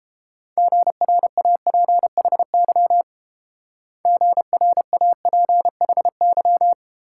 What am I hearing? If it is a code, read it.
Morse code, heard as a keyed tone, about 27 wpm, 700 Hz. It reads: GRAPHY GRAPHY
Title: Speed-Racing Multi-letter Phonemes - 22wpm